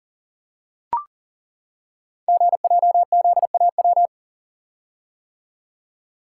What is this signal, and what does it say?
Morse code, heard as a keyed tone, about 39 wpm, 690 Hz. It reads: GJZAW
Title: Code Groups - 5 Letters - 40wpm